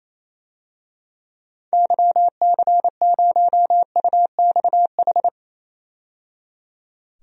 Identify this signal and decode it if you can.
Morse code, heard as a keyed tone, about 28 wpm, 700 Hz. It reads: YC0UXH
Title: DXCC-100 Call Signs - 6 Characters - 28wpm